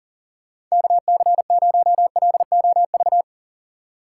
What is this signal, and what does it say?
Morse code, heard as a keyed tone, about 40 wpm, 700 Hz. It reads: KC0LOV
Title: USA Call Signs - 3x Character Spacing - 40wpm